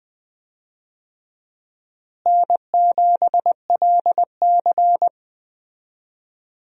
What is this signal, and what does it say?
Morse code, heard as a keyed tone, about 20 wpm, 700 Hz. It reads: N7LC
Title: USA Call Signs - 4 Characters - Repeated 2x - 20wpm